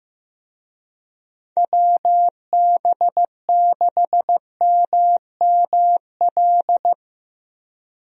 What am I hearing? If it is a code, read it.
Morse code, heard as a keyed tone, about 15 wpm, 700 Hz. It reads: WB6MML
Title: USA Call Signs - 6 Characters - Repeated 3x - 15wpm